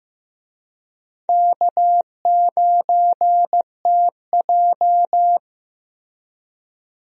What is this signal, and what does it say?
Morse code, heard as a keyed tone, about 16 wpm, 700 Hz. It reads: K9TJ
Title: USA Call Signs - 4 Characters - Repeated 2x - 15wpm